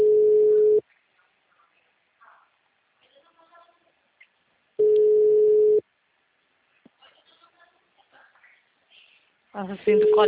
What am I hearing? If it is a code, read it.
Ringback tone (Europe, 425 Hz), with speech.